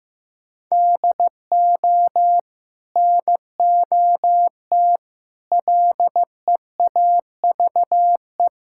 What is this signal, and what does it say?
Morse code, heard as a keyed tone, about 16 wpm, 700 Hz. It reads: DO NOT LEAVE
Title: Sentences from Top 1600 Words 15wpm